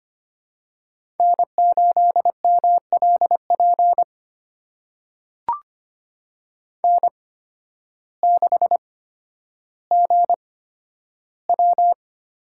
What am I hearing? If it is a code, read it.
Morse code, heard as a keyed tone, about 25 wpm, spaced as for 12 wpm, 700 Hz. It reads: N8MLP N6GW
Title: USA Call Signs - 5 Characters - 8x Character Spacing - 25wpm